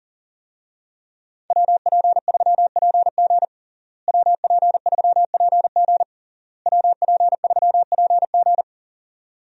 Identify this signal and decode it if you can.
Morse code, heard as a keyed tone, about 37 wpm, 700 Hz. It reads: WP3PG WP3PG WP3PG
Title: USA Call Signs - 5 Characters - Repeated 3x - 40wpm